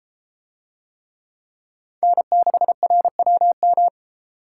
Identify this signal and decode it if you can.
Morse code, heard as a keyed tone, about 33 wpm, 700 Hz. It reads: N6RWM